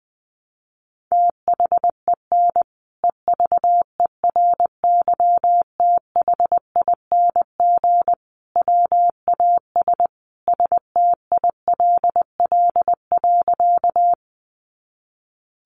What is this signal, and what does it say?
Morse code, heard as a keyed tone, about 20 wpm, 700 Hz. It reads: THEN EVERYTHING WAS STILL.
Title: Sentences from Top 700 Words 20wpm